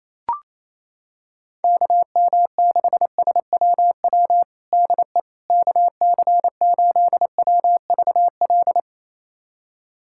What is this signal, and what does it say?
Morse code, heard as a keyed tone, about 28 wpm, 700 Hz. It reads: KM6SWW DE KC8WVL
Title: USA Callsign DE Callsign 28wpm